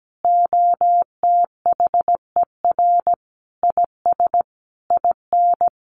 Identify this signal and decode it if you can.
Morse code, heard as a keyed tone, about 17 wpm, 700 Hz. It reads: OTHER IS IN
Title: Sentences from Top 1500 Words 17wpm